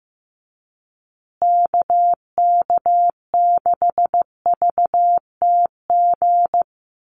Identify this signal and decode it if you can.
Morse code, heard as a keyed tone, about 15 wpm, 700 Hz. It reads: KK6VTG
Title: USA Call Signs Repeated 2x 15wpm